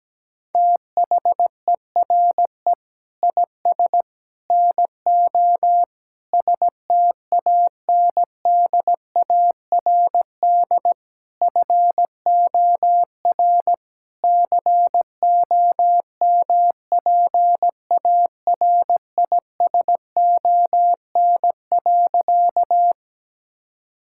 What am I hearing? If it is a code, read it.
Morse code, heard as a keyed tone, about 17 wpm, 700 Hz. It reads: THERE IS NO STANDARD FOR COMPARISON.
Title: Sentences from Top 1600 Words 17wpm